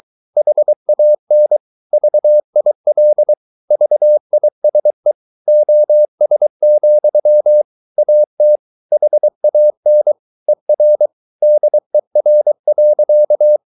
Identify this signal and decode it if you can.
Morse code, heard as a keyed tone, about 23 wpm, 600 Hz. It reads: HAN VIL VISE OS, AT HAN ER DER.
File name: Ferie QRS 22wpm